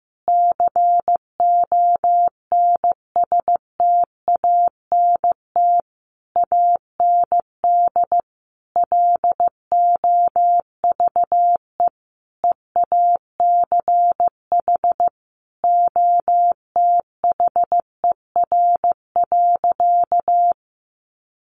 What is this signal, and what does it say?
Morse code, heard as a keyed tone, about 14 wpm, 700 Hz. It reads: CONSTANT AND LOVE EACH OTHER.
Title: Sentences from Top 1500 Words 15wpm